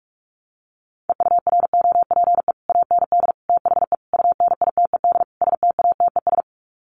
Morse code, heard as a keyed tone, about 44 wpm, 700 Hz. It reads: EUROPE AND THE UNITED STATES
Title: Sets of 5 Words Encore2 45wpm